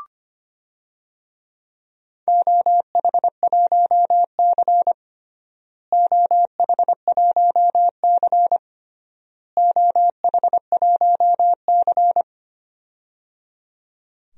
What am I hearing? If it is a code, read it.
Morse code, heard as a keyed tone, about 25 wpm, 700 Hz. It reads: OH1C OH1C OH1C
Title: DXCC-100 Call Signs - 4 Characters - Repeated 3x - 25wpm